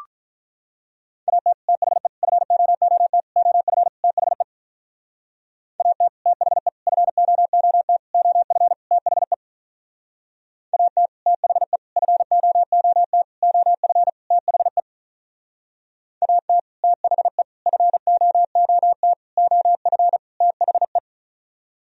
Morse code, heard as a keyed tone, about 45 wpm, 700 Hz. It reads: AT THE FOOT OF THE AT THE FOOT OF THE AT THE FOOT OF THE AT THE FOOT OF THE